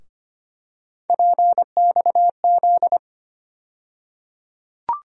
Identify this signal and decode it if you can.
Morse code, heard as a keyed tone, about 25 wpm, 700 Hz. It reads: PXZ
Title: USA Call Sign Postfixes - 3 Letters 25wpm